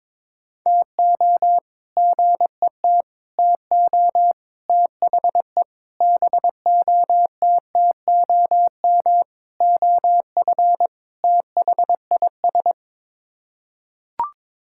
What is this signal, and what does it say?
Morse code, heard as a keyed tone, about 22 wpm, 700 Hz. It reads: TO GET TO THE BOTTOM OF THIS
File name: practice / 22wpm / Sets of 7 Words - 22wpm